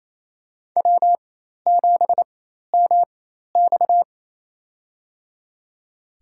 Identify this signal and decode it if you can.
Morse code, heard as a keyed tone, about 27 wpm, spaced as for 13 wpm, 700 Hz. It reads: W7MX
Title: USA Call Signs - 4 Characters - 4x Character Spacing - 28wpm